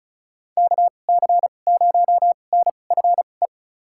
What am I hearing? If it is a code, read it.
Morse code, heard as a keyed tone, about 36 wpm, spaced as for 25 wpm, 700 Hz. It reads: KC0NFE
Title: USA Call Signs - 2x Character Spacing - 35wpm